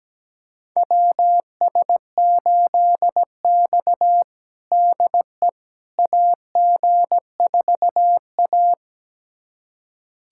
Morse code, heard as a keyed tone, about 17 wpm, 700 Hz. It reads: WS8X DE AG4A
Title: USA Callsign DE Callsign - 4 Characters - 17wpm